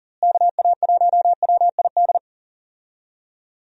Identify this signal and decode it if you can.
Morse code, heard as a keyed tone, about 40 wpm, 700 Hz. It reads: KA1WID
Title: USA Call Signs - 6 Characters - Repeated 3x - 40wpm